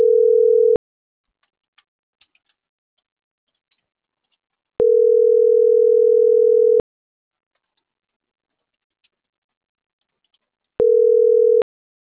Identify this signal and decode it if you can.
Ringback tone (North America), 440 + 480 Hz.